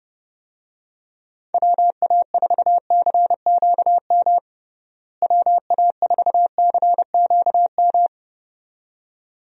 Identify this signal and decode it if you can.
Morse code, heard as a keyed tone, about 30 wpm, 700 Hz. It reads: WA4CQM WA4CQM
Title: USA Call Signs - 6 Characters - Repeated 2x - 30wpm